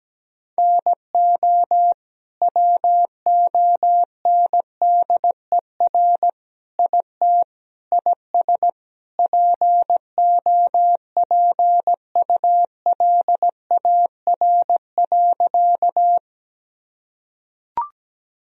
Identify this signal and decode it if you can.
Morse code, heard as a keyed tone, about 17 wpm, 700 Hz. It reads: NO WONDER IT IS POPULAR.
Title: Sentences from Top 1000 Words - Review Part 3 - 17wpm